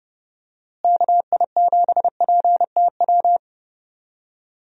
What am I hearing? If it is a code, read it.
Morse code, heard as a keyed tone, about 30 wpm, 700 Hz. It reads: KI7PTW
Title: USA Call Signs - 6 Characters - Repeated 2x - 30wpm